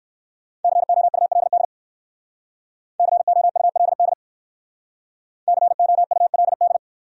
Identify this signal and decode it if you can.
Morse code, heard as a keyed tone, about 68 wpm, 700 Hz. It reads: COULD COULD COULD